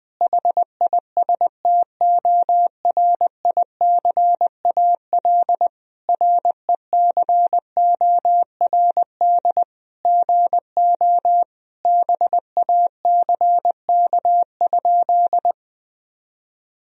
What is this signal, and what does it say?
Morse code, heard as a keyed tone, about 20 wpm, 700 Hz. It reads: HISTORICAL RECORD GO BACK?
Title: Sentences from Top 1000 Words - Review Part 4 - 20wpm